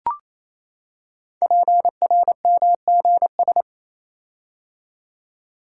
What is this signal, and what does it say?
Morse code, heard as a keyed tone, about 27 wpm, 700 Hz. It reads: PRMGS
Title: Code Groups - 5 Letters - 28wpm